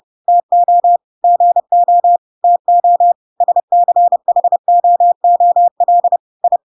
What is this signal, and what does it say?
Morse code, heard as a keyed tone, about 30 wpm, 700 Hz. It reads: TO GO TO SCHOOL I
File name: Pinocchio / Pinocchio ch8 30 30 ik1hge